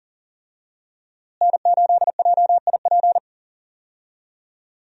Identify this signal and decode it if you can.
Morse code, heard as a keyed tone, about 40 wpm, 700 Hz. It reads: N8JIP
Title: USA Call Signs - 5 Characters - Repeated 2x - 40wpm